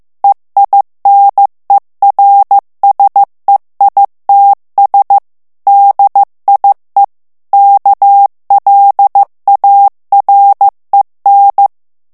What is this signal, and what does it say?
Morse code, heard as a keyed tone, about 15 wpm, 800 Hz. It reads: EINERSEITS DIE KLAREN
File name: morse mp3 / ALLEMAND texte clair 1 vitesse 14 mots minute